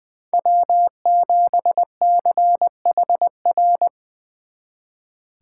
Morse code, heard as a keyed tone, about 24 wpm, 700 Hz. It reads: W7CHR
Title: Speed Racing - USA 1x3 Call Signs 20wpm